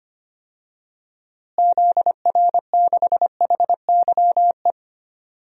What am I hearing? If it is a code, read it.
Morse code, heard as a keyed tone, about 25 wpm, 700 Hz. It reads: ZR6HYE
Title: DXCC-100 Call Signs - 6 Characters - Repeated 2x - 25wpm